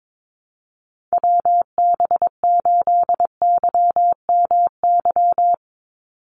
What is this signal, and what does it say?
Morse code, heard as a keyed tone, about 22 wpm, 700 Hz. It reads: WB8YMY
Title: USA Call Signs 22wpm